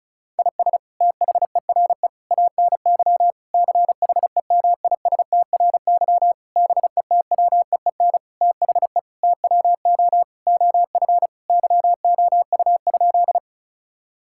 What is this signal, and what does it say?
Morse code, heard as a keyed tone, about 34 wpm, 700 Hz. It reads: IS THERE ANY CHEMISTRY BETWEEN THE TWO OF YOU?
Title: Sentences from Top 1000 Words - Review Part 6 - 35wpm